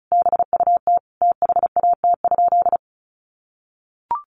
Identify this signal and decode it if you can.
Morse code, heard as a keyed tone, about 34 wpm, 700 Hz. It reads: BUT THAT?
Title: Sentences from Top 1400 Words 35wpm